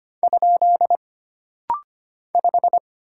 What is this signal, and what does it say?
Morse code, heard as a keyed tone, about 25 wpm, 700 Hz. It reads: ? 5
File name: Single Character (Letter:Number:Punctuation:Prosign) - Rapid-Fire - 25wpm